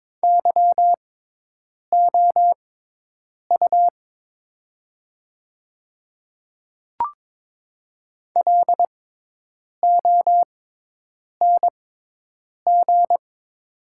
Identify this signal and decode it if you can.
Morse code, heard as a keyed tone, about 22 wpm, spaced as for 8 wpm, 700 Hz. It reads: YOU LONG
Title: 6x Character Spacing - Top 200 Words - 22wpm